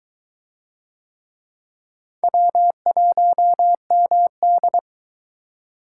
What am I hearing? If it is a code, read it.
Morse code, heard as a keyed tone, about 22 wpm, 700 Hz. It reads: W1MD